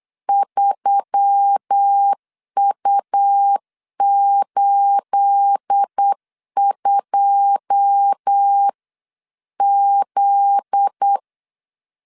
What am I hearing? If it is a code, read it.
Morse code, heard as a keyed tone, about 8 wpm, 800 Hz. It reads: 3U82Z
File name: CW / DET / 1 9 1500C S C 8wpm 5653